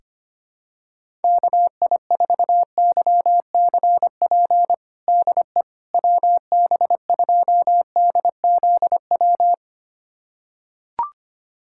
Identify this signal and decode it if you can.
Morse code, heard as a keyed tone, about 25 wpm, 700 Hz. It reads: KI4YCP DE WB2DZW
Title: USA Callsign DE Callsign - 6 Characters - 25wpm